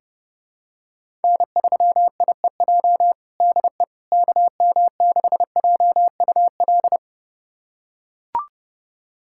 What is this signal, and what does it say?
Morse code, heard as a keyed tone, about 32 wpm, 700 Hz. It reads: N3IEJ DE KM6JUL
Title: USA Callsign DE Callsign 30wpm